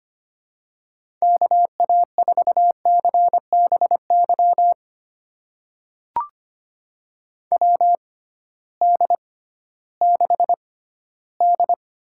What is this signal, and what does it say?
Morse code, heard as a keyed tone, about 25 wpm, spaced as for 15 wpm, 700 Hz. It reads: KA4CBY WD6D
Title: USA Call Signs - 6 Characters - 6x Character Spacing - 25wpm